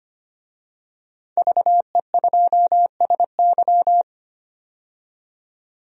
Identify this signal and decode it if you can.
Morse code, heard as a keyed tone, about 25 wpm, 700 Hz. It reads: VE2SY
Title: DXCC-100 Call Signs - 5 Characters - 25wpm